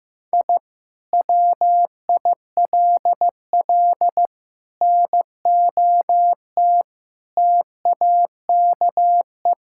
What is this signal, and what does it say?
Morse code, heard as a keyed tone, about 15 wpm, 700 Hz. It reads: I WILL NOT TAKE
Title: Sentences from Top 1000 Words - Review Part 6 - 15wpm